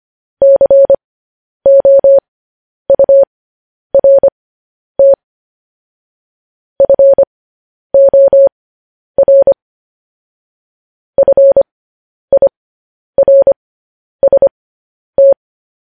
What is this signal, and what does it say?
Morse code, heard as a keyed tone, about 25 wpm, spaced as for 10 wpm, 550 Hz. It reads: COURT FOR FIRST
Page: news-headlines-in-morse-code-at-10-wpm